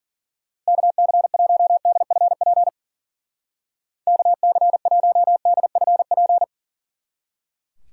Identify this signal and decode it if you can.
Morse code, heard as a keyed tone, about 40 wpm, 700 Hz. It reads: KC1DFP KC1DFP
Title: Speed Racing - USA 2x3 Call Signs 40wpm